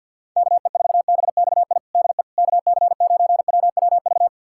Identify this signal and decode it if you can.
Morse code, heard as a keyed tone, about 51 wpm, 700 Hz. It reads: KE4BXI DE KC9WWV
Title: USA Callsign DE Callsign 50wpm